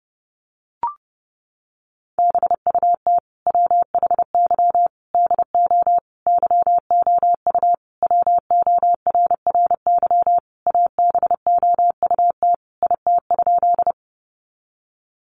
Morse code, heard as a keyed tone, about 30 wpm, 700 Hz. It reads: BUT WHY DO YOU WORRY ABOUT IT?
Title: Sentences from Top 1100 Words 30wpm